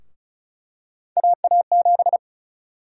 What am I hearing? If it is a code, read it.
Morse code, heard as a keyed tone, about 36 wpm, 700 Hz. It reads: AA7
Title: USA Call Sign Prefixes - 35wpm